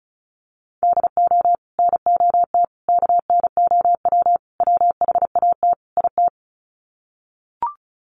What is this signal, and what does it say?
Morse code, heard as a keyed tone, about 34 wpm, 700 Hz. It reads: DO NOT KNOW WHAT IT
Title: Sets of 5 Words 35wpm